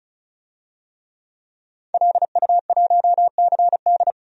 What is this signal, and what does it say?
Morse code, heard as a keyed tone, about 34 wpm, 700 Hz. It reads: LU1CD
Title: DXCC-100 Call Signs - 5 Characters - 35wpm